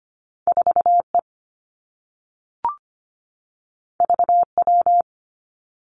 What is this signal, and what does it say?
Morse code, heard as a keyed tone, about 25 wpm, 700 Hz. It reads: 4E VW
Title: Two Challenging Letter-Number Combinations 25wpm